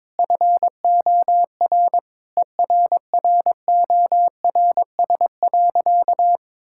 Morse code, heard as a keyed tone, about 22 wpm, 700 Hz. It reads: FOR ERRORS.
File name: Difficult Sentences with Hints - 22wpm